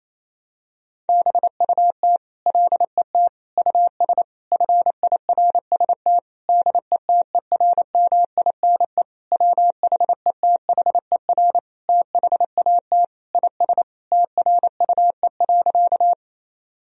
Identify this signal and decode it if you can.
Morse code, heard as a keyed tone, about 28 wpm, 700 Hz. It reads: BUT LET US FIRST DETERMINE WHETHER THAT IS TRUE.
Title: Sentences from Top 1000 Words - Review Part 4 - 28wpm